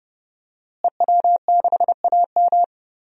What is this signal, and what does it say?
Morse code, heard as a keyed tone, about 30 wpm, 700 Hz. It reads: EW6AM
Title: DXCC-100 Call Signs - 5 Characters - Repeated 3x - 30wpm